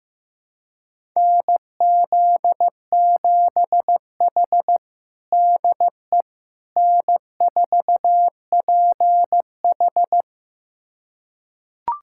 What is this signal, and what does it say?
Morse code, heard as a keyed tone, about 15 wpm, 700 Hz. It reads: NZ7H DE N4PH
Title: USA Callsign DE Callsign - 4 Characters - 15wpm